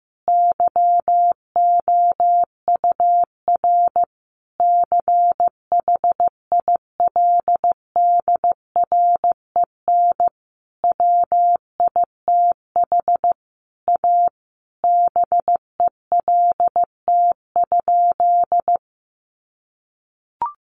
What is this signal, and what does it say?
Morse code, heard as a keyed tone, about 15 wpm, 700 Hz. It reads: YOUR CHILDREN WITH A BELT?
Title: Sentences from Top 1400 Words 15wpm